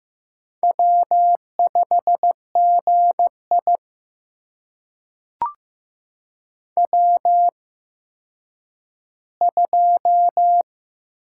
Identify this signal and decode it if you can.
Morse code, heard as a keyed tone, about 15 wpm, spaced as for 9 wpm, 700 Hz. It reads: W5GI W2
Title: USA Call Signs - 4 Characters - 8x Character Spacing - 15wpm